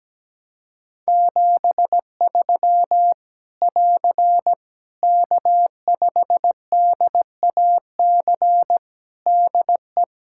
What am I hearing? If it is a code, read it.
Morse code, heard as a keyed tone, about 17 wpm, 700 Hz. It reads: 73 <AR> K5DAC DE
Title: Practice QSO - Exchange 4 - Ending - 17wpm